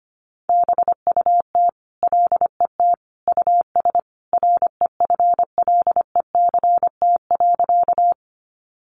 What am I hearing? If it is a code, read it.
Morse code, heard as a keyed tone, about 25 wpm, 700 Hz. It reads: BUT LET US REFLECT.
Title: Sentences from Top 1500 Words 25wpm